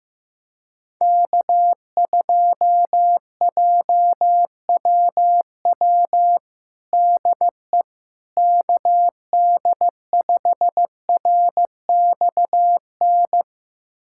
Morse code, heard as a keyed tone, about 15 wpm, 700 Hz. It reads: K2JWW DE KD5RXN
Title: USA Callsign DE Callsign 15wpm